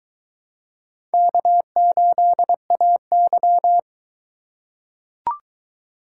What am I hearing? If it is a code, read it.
Morse code, heard as a keyed tone, about 22 wpm, 700 Hz. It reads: K8AY